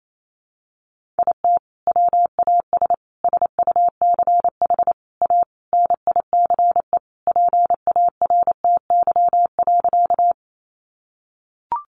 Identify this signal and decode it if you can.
Morse code, heard as a keyed tone, about 28 wpm, 700 Hz. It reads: IT WAS SUCH A NICE PARTY.
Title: Sentences from Top 1400 Words 28wpm